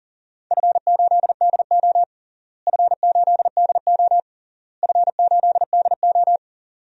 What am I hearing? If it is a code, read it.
Morse code, heard as a keyed tone, about 40 wpm, 700 Hz. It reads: F8DO F8DO F8DO
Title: DXCC-100 Call Signs - 4 Characters - Repeated 3x - 40wpm